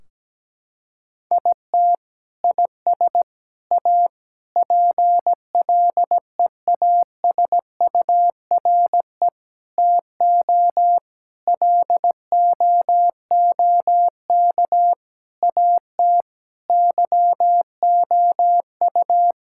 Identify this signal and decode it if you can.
Morse code, heard as a keyed tone, about 17 wpm, 700 Hz. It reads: IT IS A PLEASURE TO LOOK AT YOU
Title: Sentences from Top 1600 Words 17wpm